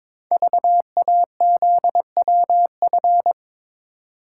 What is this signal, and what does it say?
Morse code, heard as a keyed tone, about 22 wpm, 700 Hz. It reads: VAZWF
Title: Code Groups - 5 Letter:Numbers - 22wpm